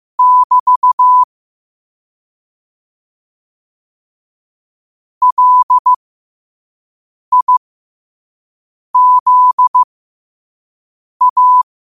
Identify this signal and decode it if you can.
Morse code, heard as a keyed tone, about 15 wpm, spaced as for 5 wpm, 1000 Hz. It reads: <BT> LIZA
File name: qotdfiles / 5wpm